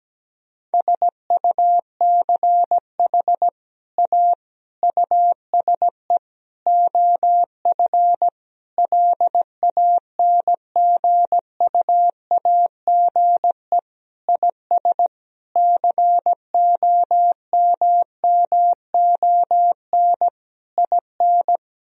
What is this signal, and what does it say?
Morse code, heard as a keyed tone, about 17 wpm, 700 Hz. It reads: SUCH A USE OF LANGUAGE IS COMMON IN
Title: Sentences from Top 1000 Words - Review Part 3 - 17wpm